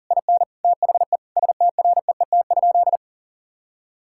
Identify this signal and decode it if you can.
Morse code, heard as a keyed tone, about 40 wpm, 700 Hz. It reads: IN THE STREET?
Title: Sentences from Top 1600 Words 40wpm